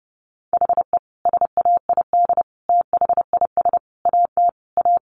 Morse code, heard as a keyed tone, about 30 wpm, 700 Hz. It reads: HE SAID THIS AT A
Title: Sentences from Top 1000 Words 30wpm